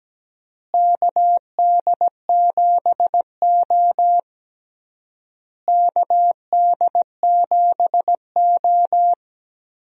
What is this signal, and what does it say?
Morse code, heard as a keyed tone, about 17 wpm, 700 Hz. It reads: KD7O KD7O
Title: USA Call Signs - 4 Characters - Repeated 3x - 17wpm